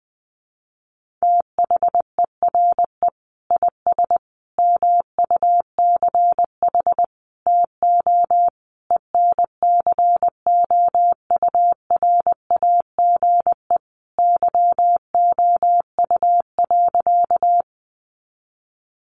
Morse code, heard as a keyed tone, about 20 wpm, 700 Hz. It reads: THERE IS MUCH TO ENCOURAGE YOU.
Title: Sentences from Top 1400 Words 20wpm